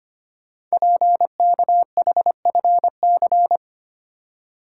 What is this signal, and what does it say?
Morse code, heard as a keyed tone, about 25 wpm, 700 Hz. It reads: PKHFC
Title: Code Groups - 5 Letters - 25wpm